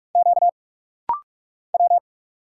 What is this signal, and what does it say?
Morse code, heard as a keyed tone, about 44 wpm, 700 Hz. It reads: Q W
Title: Single Letter-Number Alt - Rapid-Fire 45wpm